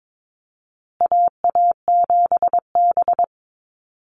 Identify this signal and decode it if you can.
Morse code, heard as a keyed tone, about 22 wpm, 700 Hz. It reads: AA7B
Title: USA Call Signs Repeated 2x 22wpm